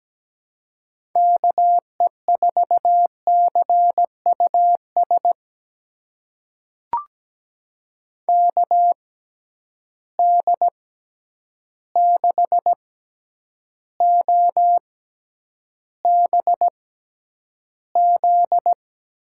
Morse code, heard as a keyed tone, about 17 wpm, spaced as for 9 wpm, 700 Hz. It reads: KE4CUS KD6OBZ
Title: USA Call Signs - 6 Characters - 6x Character Spacing - 17wpm